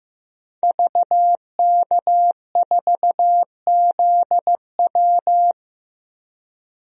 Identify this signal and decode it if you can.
Morse code, heard as a keyed tone, about 15 wpm, 700 Hz. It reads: VK4ZW
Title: DXCC-100 Call Signs - 5 Characters - 15wpm